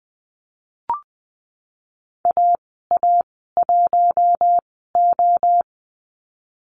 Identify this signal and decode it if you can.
Morse code, heard as a keyed tone, about 20 wpm, spaced as for 15 wpm, 700 Hz. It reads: AA1O
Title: 042 Lesson 8g - Number 1 - Call Signs 20wpm